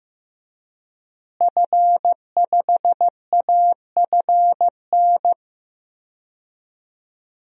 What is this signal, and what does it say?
Morse code, heard as a keyed tone, about 15 wpm, 700 Hz. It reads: F5AFN
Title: DXCC-100 Call Signs - 5 Characters - Repeated 3x - 15wpm